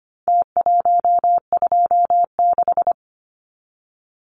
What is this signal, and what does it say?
Morse code, heard as a keyed tone, about 25 wpm, 700 Hz. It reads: T126
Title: Numbers - Contest Style - 4-Digit 25wpm